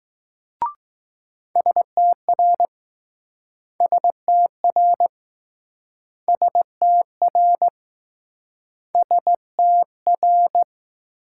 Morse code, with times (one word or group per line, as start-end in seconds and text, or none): 1.6-2.7 STR
3.8-5.1 STR
6.3-7.7 STR
8.9-10.6 STR